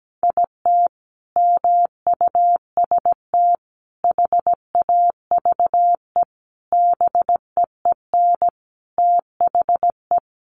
Morse code, heard as a keyed tone, about 17 wpm, 700 Hz. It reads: IT MUST HAVE BEEN THE
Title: Sentences from Top 800 Words 17wpm